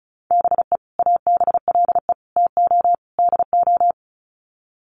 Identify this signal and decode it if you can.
Morse code, heard as a keyed tone, about 34 wpm, 700 Hz. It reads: BE ABLE TO DO
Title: Sets of 4 Words 35wpm